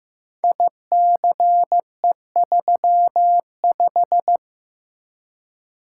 Morse code, heard as a keyed tone, about 15 wpm, 700 Hz. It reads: ICE35
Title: Code Groups - 5 Letter:Numbers - 15wpm